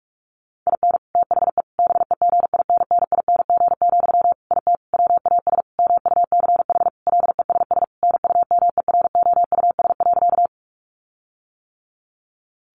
Morse code, heard as a keyed tone, about 44 wpm, 700 Hz. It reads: IN THE BEGINNING, IT WAS MUCH LESS NUMEROUS.
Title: Sentences from Top 1400 Words 45wpm